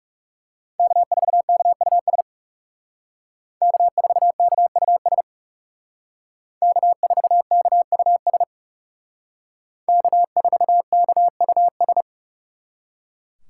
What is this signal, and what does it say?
Morse code, heard as a keyed tone, about 38 wpm, 700 Hz. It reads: K4KUS K4KUS K4KUS K4KUS